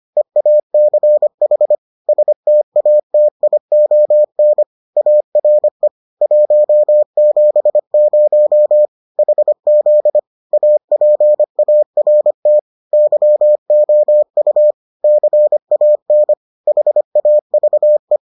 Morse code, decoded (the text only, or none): EACH STATION ARE 170 HZ APART YOU CAN HAVE